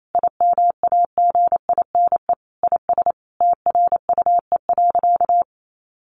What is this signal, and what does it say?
Morse code, heard as a keyed tone, about 28 wpm, 700 Hz. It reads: IMAGINE IS TRUE.
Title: Sentences from Top 1400 Words 28wpm